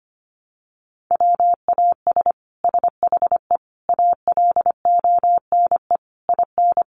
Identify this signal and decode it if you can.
Morse code, heard as a keyed tone, about 25 wpm, 700 Hz. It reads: WAS SHE ALONE IN
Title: Sentences from Top 1100 Words 25wpm